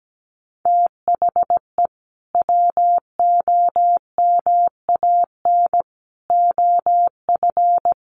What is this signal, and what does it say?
Morse code, heard as a keyed tone, about 17 wpm, 700 Hz. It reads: THE WOMAN OF
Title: Sentences from Top 1500 Words 17wpm